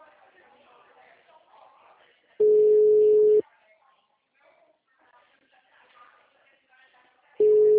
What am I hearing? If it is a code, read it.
Ringback tone (Europe, 425 Hz), with speech.